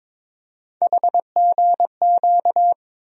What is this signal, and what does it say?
Morse code, heard as a keyed tone, about 22 wpm, 700 Hz. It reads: HGQ
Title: USA Call Sign Postfixes - 3 Letters 22wpm